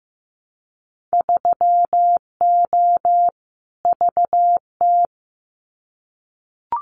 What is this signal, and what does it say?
Morse code, heard as a keyed tone, about 15 wpm, 700 Hz. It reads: 3O VT
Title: Winter Field Day Exchanges Encore 15wpm